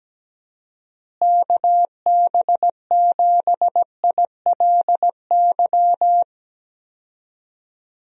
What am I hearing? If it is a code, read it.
Morse code, heard as a keyed tone, about 17 wpm, 700 Hz. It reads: KB7ILY
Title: USA Call Signs - 6 Characters - Repeated 3x - 17wpm